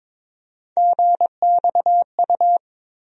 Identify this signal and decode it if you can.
Morse code, heard as a keyed tone, about 22 wpm, 700 Hz. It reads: GXU
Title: USA Call Sign Postfixes - 3 Letters 22wpm